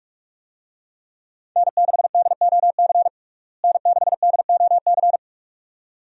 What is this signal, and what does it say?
Morse code, heard as a keyed tone, about 42 wpm, 700 Hz. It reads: N6DOC N6DOC
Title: USA Call Signs - 5 Characters - Repeated 2x - 45wpm